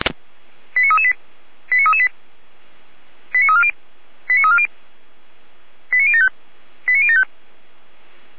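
Five-tone selective call (ZVEI), heard as quick runs of five tones, tones of about 70 ms. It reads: 89338 89338 89340 89340 89075 89075
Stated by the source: ZVEI five-tone selective call received on SDR.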